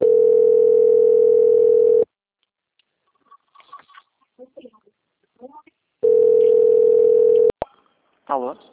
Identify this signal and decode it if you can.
Ringback tone (North America), 440 + 480 Hz, with speech.
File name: out- - - - - .7026